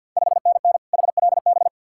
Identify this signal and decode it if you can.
Morse code, heard as a keyed tone, about 51 wpm, 690 Hz. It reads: 5NN HLB
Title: Ontario QSO Party Exchange 50wpm